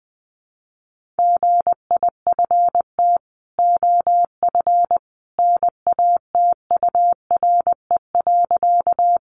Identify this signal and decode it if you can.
Morse code, heard as a keyed tone, about 20 wpm, 700 Hz. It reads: GIFT OF NATURE.
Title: Sentences from Top 900 Words 20wpm